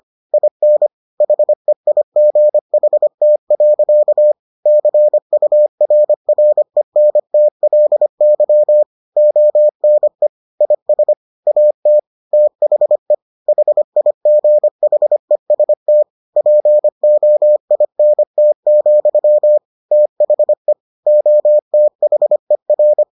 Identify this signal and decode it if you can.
Morse code, heard as a keyed tone, about 25 wpm, 600 Hz. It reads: IN HEIGHT. CURRENTLY ONE IS AT THE HIGHEST POINT, THE OTHER